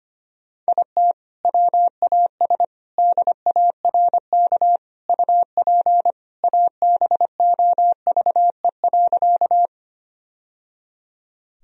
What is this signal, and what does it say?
Morse code, heard as a keyed tone, about 25 wpm, 700 Hz. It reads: IT WAS DARK UP ABOVE.
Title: Sentences from Top 1000 Words - Review Part 2 - 25wpm